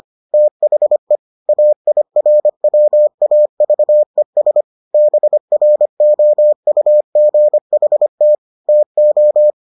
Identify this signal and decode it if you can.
Morse code, heard as a keyed tone, about 25 wpm, 600 Hz. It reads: THE AIRWAVES BROUGHT TO